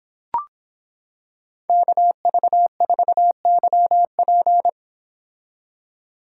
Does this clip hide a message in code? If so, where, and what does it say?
Yes: Morse code from 1.7 to 4.7 s, heard as a keyed tone, about 25 wpm, 700 Hz. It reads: KV4YP